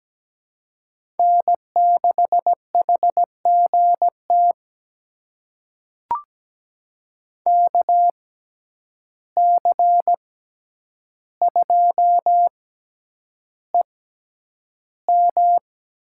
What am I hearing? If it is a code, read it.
Morse code, heard as a keyed tone, about 17 wpm, spaced as for 9 wpm, 700 Hz. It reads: N6HGT KC2EM
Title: USA Call Signs - 6x Character Spacing - 17wpm